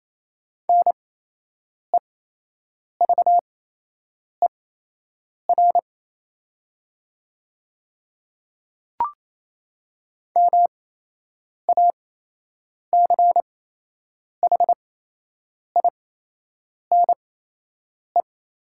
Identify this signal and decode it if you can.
Morse code, heard as a keyed tone, about 28 wpm, spaced as for 8 wpm, 700 Hz. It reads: NEVER MACHINE
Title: 8x Character Spacing - Top 500 Words - 28wpm